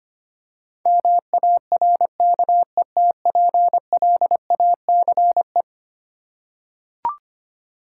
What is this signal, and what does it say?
Morse code, heard as a keyed tone, about 25 wpm, 700 Hz. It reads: MARKETPLACE
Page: morse-code-ninja-practice-25wpm